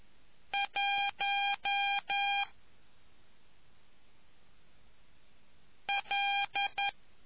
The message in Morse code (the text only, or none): 1 L